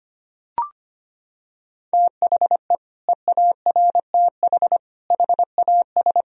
Morse code, heard as a keyed tone, about 25 wpm, 700 Hz. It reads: THE EARTH HAS
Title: Sentences from Top 1000 Words - Review Part 2 - 25wpm